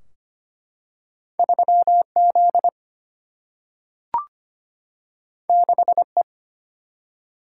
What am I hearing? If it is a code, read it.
Morse code, heard as a keyed tone, about 25 wpm, 700 Hz. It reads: 3 Z 6 E
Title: Single-Letter Single-Number 25wpm